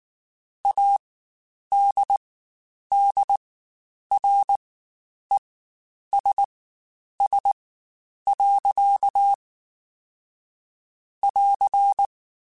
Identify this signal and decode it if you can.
Morse code, heard as a keyed tone, about 19 wpm, spaced as for 9 wpm, 800 Hz. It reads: ADDRESS. <AR>
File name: Spaced / Chuckles Spaced-mp3s / Temptation sp @18